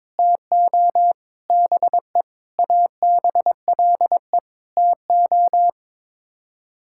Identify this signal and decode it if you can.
Morse code, heard as a keyed tone, about 21 wpm, 700 Hz. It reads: TO BE ABLE TO
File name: practice / 22wpm / Sets of 7 Words - 22wpm